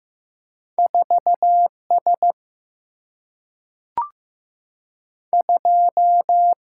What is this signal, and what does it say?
Morse code, heard as a keyed tone, about 15 wpm, 700 Hz. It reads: 4 S 2
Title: Single-Letter Single-Number 15wpm